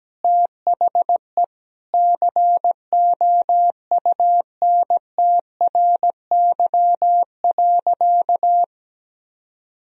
Morse code, heard as a keyed tone, about 17 wpm, 700 Hz. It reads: THE COUNTRY.
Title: Sentences from Top 300 Words and Abbreviations 17wpm